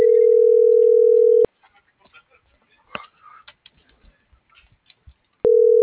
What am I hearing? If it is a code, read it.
Ringback tone (North America), 440 + 480 Hz.